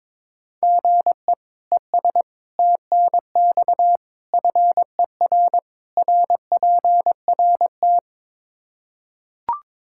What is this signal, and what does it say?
Morse code, heard as a keyed tone, about 22 wpm, 700 Hz. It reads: GE ES TNX FER RPRT
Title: Practice QSO - Exchange 1 - Beginning - 22wpm